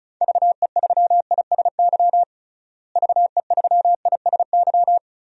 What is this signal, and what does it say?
Morse code, heard as a keyed tone, about 36 wpm, 700 Hz. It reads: VE3ISY VE3ISY
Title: DXCC-100 Call Signs - 6 Characters - Repeated 2x - 35wpm